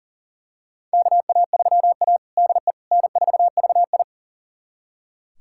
Morse code, heard as a keyed tone, about 40 wpm, 700 Hz. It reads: KA3A DE N4VI
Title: USA Callsign DE Callsign - 4 Characters - 40wpm